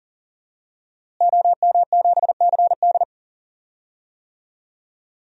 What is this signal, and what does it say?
Morse code, heard as a keyed tone, about 40 wpm, 700 Hz. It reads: OM7CD
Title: DXCC-100 Call Signs - 5 Characters - Repeated 3x - 40wpm